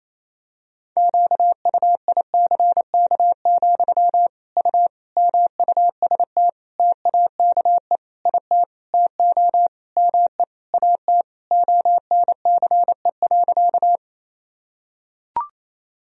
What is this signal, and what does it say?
Morse code, heard as a keyed tone, about 27 wpm, 700 Hz. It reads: QUICK, U MUST TAKE IT TO ME AT ONCE.
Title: Sentences from Top 500 Words and Abbreviations 28wpm